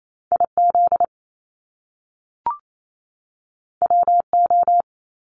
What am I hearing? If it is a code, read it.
Morse code, heard as a keyed tone, about 28 wpm, 700 Hz. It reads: IZ WO
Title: Top Two Letter Combinations in Words 28wpm